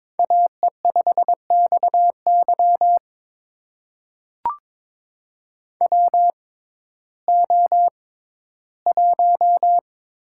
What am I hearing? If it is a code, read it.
Morse code, heard as a keyed tone, about 22 wpm, spaced as for 13 wpm, 700 Hz. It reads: AE5XY WO1
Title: USA Call Signs - 6x Character Spacing - 22wpm